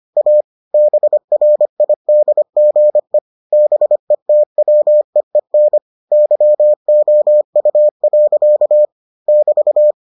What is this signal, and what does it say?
Morse code, heard as a keyed tone, about 25 wpm, 600 Hz. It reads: A BRIDGE BETWEEN YOU. <BT>